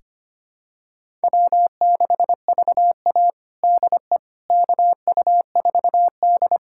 Morse code, heard as a keyed tone, about 25 wpm, 700 Hz. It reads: W6VA DE KU4D
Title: USA Callsign DE Callsign - 4 Characters - 25wpm